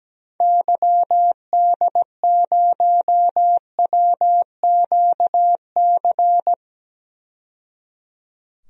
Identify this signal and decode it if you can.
Morse code, heard as a keyed tone, about 17 wpm, 700 Hz. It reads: YD0WQC
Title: DXCC-100 Call Signs - 6 Characters - 17wpm